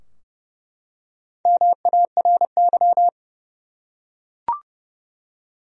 Morse code, heard as a keyed tone, about 31 wpm, 700 Hz. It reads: MARY
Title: USA Common Names - 30wpm